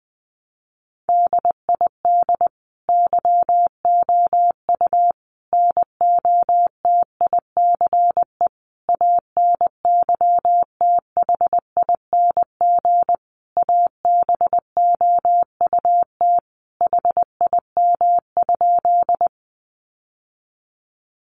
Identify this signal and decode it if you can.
Morse code, heard as a keyed tone, about 20 wpm, 700 Hz. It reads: DID YOU NOTICE ANYTHING ABOUT HIM?
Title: Sentences from Top 1000 Words 20wpm